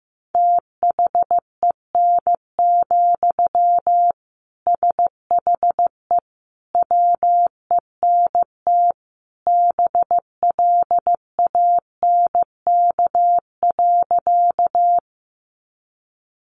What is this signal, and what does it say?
Morse code, heard as a keyed tone, about 15 wpm, 700 Hz. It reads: THEN, SHE WENT BLANK.
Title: Sentences from Top 1500 Words 15wpm